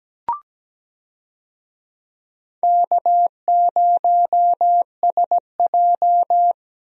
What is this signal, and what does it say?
Morse code, heard as a keyed tone, about 17 wpm, 700 Hz. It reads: K0SJ
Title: USA Call Signs - 4 Characters - 17wpm